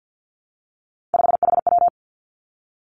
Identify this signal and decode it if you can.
Morse code, heard as a keyed tone, about 50 wpm, 700 Hz. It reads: 5HW
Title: Three Challenging Letter-Number Combinations 50wpm